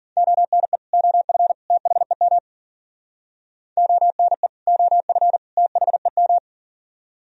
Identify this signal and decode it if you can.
Morse code, heard as a keyed tone, about 50 wpm, 700 Hz. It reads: ONE OF THEM ONE OF THEM
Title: Speed-Racing - Sets of 3 Words 40wpm